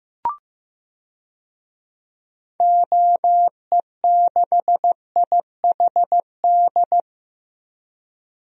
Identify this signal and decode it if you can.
Morse code, heard as a keyed tone, about 15 wpm, 700 Hz. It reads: OE6IHD
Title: DXCC-100 Call Signs - 6 Characters - Repeated 2x - 15wpm